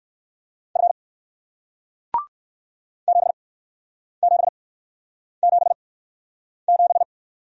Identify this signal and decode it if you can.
Morse code, heard as a keyed tone, about 60 wpm, spaced as for 9 wpm, 700 Hz. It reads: F 7777